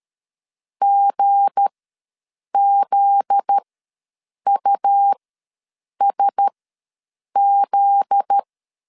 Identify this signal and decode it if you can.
Morse code, heard as a keyed tone, about 13 wpm, spaced as for 7 wpm, 800 Hz. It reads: GZUSZ
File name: 7 GF9Z ws 12wpm